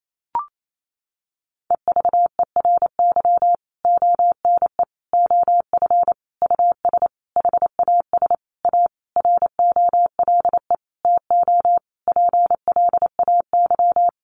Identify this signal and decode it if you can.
Morse code, heard as a keyed tone, about 28 wpm, 700 Hz. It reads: EVERY ONE OF US HAS A ROLE TO PLAY
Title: Sentences from Top 1500 Words 28wpm